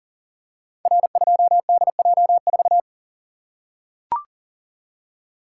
Morse code, heard as a keyed tone, about 40 wpm, 700 Hz. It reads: R2DJ4
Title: Code Groups - 5 Letter:Numbers - 40wpm